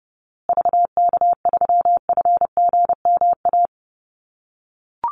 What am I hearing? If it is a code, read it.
Morse code, heard as a keyed tone, about 29 wpm, 700 Hz. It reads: VK3FGMA
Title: USA Call Signs Repeated 2x 30wpm